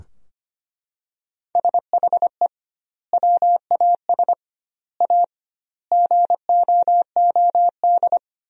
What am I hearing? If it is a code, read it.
Morse code, heard as a keyed tone, about 25 wpm, 700 Hz. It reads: SHE WAS A GOOD
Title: 2x Word-Spacing - Sentences from Top 400 Words - 25wpm